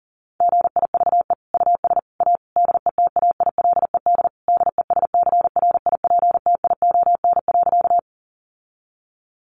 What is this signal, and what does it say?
Morse code, heard as a keyed tone, about 40 wpm, 700 Hz. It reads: GIVE US A DETAILED DESCRIPTION.
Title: Sentences from Top 1500 Words 40wpm